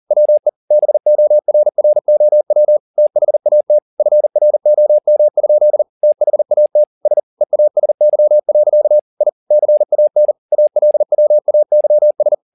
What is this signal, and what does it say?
Morse code, heard as a keyed tone, about 40 wpm, 600 Hz. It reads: WE BORROW THAT FROM? THAT S EASY. I CAN ALWAYS